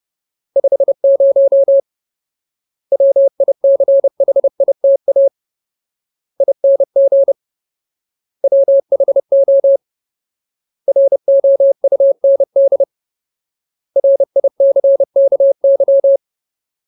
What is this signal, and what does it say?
Morse code, heard as a keyed tone, about 30 wpm, 550 Hz. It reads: 50 WICHITA ING WHO ROUND RICKY
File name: Ultimate Word List / 30WPM Ultimate